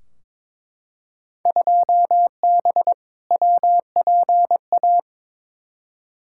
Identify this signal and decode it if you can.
Morse code, heard as a keyed tone, about 21 wpm, 700 Hz. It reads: 2B WPA
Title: Field Day Exchanges Encore - 22wpm